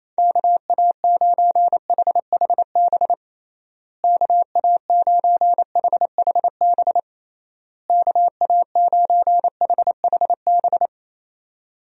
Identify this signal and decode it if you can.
Morse code, heard as a keyed tone, about 27 wpm, 700 Hz. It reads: KA9HHB KA9HHB KA9HHB
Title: USA Call Signs - 6 Characters - Repeated 3x - 28wpm